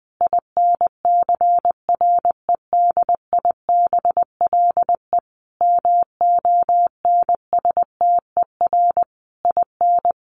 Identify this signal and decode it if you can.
Morse code, heard as a keyed tone, about 20 wpm, 700 Hz. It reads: INCREDIBLE MONSTER IN
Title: A Sound Of Thunder 20wpm